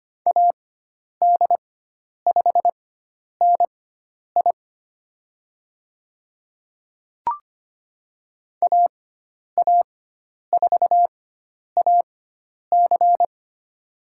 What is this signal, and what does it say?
Morse code, heard as a keyed tone, about 25 wpm, spaced as for 10 wpm, 700 Hz. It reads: AD5NI AA4AC
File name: Lesson 16g - Letter c - Call Signs 25wpm